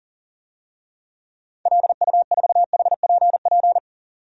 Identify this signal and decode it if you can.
Morse code, heard as a keyed tone, about 40 wpm, 700 Hz. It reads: LU4HPP
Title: DXCC-100 Call Signs - 6 Characters - 40wpm